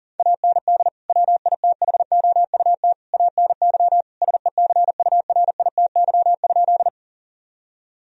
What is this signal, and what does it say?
Morse code, heard as a keyed tone, about 37 wpm, 690 Hz. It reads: AND WITHOUT ANY SECURITY?
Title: Sentences from Top 1000 Words - Review Part 4 - 40wpm